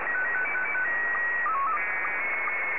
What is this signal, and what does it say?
SSTV (slow-scan television), Scottie 1.